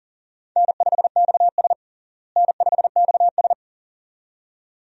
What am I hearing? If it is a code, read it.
Morse code, heard as a keyed tone, about 37 wpm, 700 Hz. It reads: N5XS N5XS
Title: USA Call Signs - 4 Characters - Repeated 2x - 40wpm